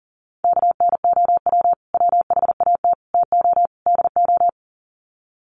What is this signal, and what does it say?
Morse code, heard as a keyed tone, about 40 wpm, 700 Hz. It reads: KNOW WHAT TO DO
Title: Sets of 4 Words 40wpm